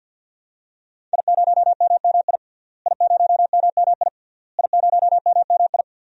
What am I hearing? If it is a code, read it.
Morse code, heard as a keyed tone, about 47 wpm, 700 Hz. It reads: I0MMI I0MMI I0MMI
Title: DXCC-100 Call Signs - 5 Characters - Repeated 3x - 50wpm